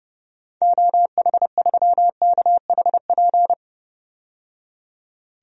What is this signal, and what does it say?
Morse code, heard as a keyed tone, about 30 wpm, 700 Hz. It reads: OH3KHP
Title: DXCC-100 Call Signs - 6 Characters - 30wpm